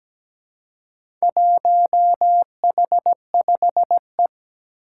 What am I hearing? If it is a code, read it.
Morse code, heard as a keyed tone, about 17 wpm, 700 Hz. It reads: 1H5E
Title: Four Challenging Letter-Number Combinations 17wpm